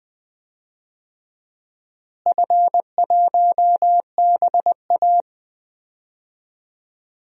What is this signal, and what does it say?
Morse code, heard as a keyed tone, about 20 wpm, 700 Hz. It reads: F1BA
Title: DXCC-100 Call Signs - 4 Characters - Repeated 3x - 20wpm